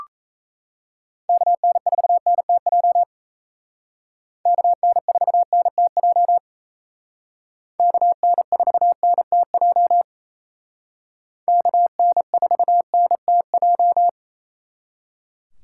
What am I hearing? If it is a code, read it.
Morse code, heard as a keyed tone, about 36 wpm, 700 Hz. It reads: KN4NTJ KN4NTJ KN4NTJ KN4NTJ